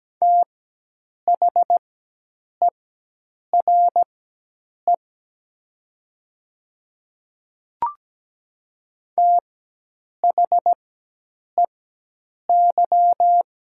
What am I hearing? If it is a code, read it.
Morse code, heard as a keyed tone, about 17 wpm, spaced as for 8 wpm, 700 Hz. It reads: THERE THEY
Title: 4x Character Spacing - Top 200 Words - 17wpm